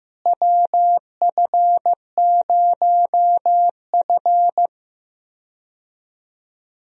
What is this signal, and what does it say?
Morse code, heard as a keyed tone, about 15 wpm, 700 Hz. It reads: WF0F
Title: USA Call Signs - 4 Characters - Repeated 2x - 15wpm